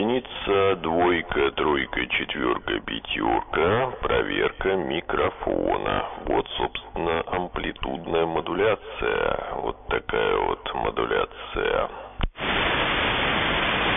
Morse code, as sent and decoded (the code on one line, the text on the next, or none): none